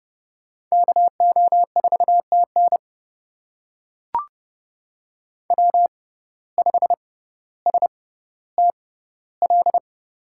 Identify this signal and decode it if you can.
Morse code, heard as a keyed tone, about 30 wpm, spaced as for 15 wpm, 700 Hz. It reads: KO4TN W5STL
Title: USA Call Signs - 5 Characters - 6x Character Spacing - 30wpm